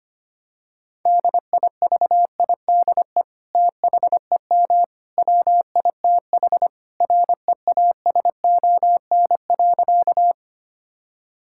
Morse code, heard as a keyed tone, about 25 wpm, 700 Hz. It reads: DIVIDE THEM WITH REASON.
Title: Sentences from Top 1600 Words 25wpm